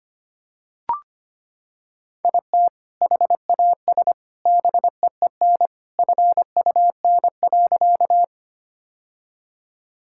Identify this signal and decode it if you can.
Morse code, heard as a keyed tone, about 25 wpm, 700 Hz. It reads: IT HAS BEEN FUN.
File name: practice / 25wpm / Sentences with 2-Word Phrases - 25wpm